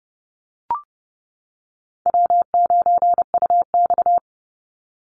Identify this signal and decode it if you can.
Morse code, heard as a keyed tone, about 30 wpm, 700 Hz. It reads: W9UX
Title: NA SOTA Chaser Callsigns 30wpm